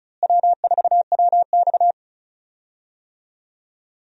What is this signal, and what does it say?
Morse code, heard as a keyed tone, about 36 wpm, 700 Hz. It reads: W4WX
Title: USA Call Signs - 4 Characters - Repeated 3x - 35wpm